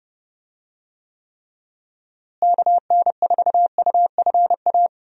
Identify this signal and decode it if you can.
Morse code, heard as a keyed tone, about 32 wpm, 700 Hz. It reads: KN4UFA
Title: USA Call Signs - 6 Characters - Repeated 3x - 30wpm